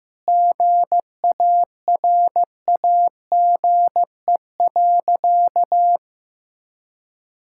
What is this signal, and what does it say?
Morse code, heard as a keyed tone, about 15 wpm, 700 Hz. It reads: GARAGE.
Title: Sentences from Top 1600 Words 15wpm